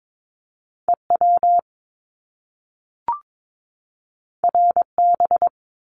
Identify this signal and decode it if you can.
Morse code, heard as a keyed tone, about 22 wpm, 700 Hz. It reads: EW RB
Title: Top Two Letter Combinations in Words 22wpm